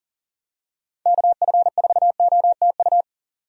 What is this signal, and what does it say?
Morse code, heard as a keyed tone, about 37 wpm, 700 Hz. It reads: KF4OTU
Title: USA Call Signs - 6 Characters - 40wpm